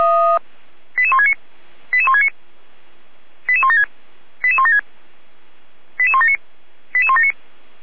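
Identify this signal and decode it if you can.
Five-tone selective call (ZVEI), heard as quick runs of five tones, tones of about 70 ms. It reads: 88169 88169 88176 88176 88179 88179
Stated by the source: Received on SDR, ZVEI five-tone selective call.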